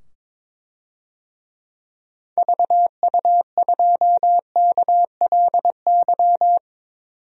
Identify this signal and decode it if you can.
Morse code, heard as a keyed tone, about 22 wpm, 700 Hz. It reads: VU2KLY
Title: DXCC-100 Call Signs - 6 Characters - 22wpm